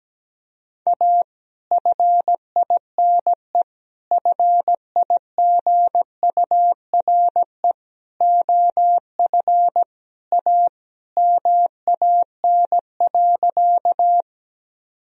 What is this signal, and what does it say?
Morse code, heard as a keyed tone, about 17 wpm, 700 Hz. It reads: A FINE FIGURE OF A MAN.
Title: Sentences from Top 1000 Words - Review Part 2 - 17wpm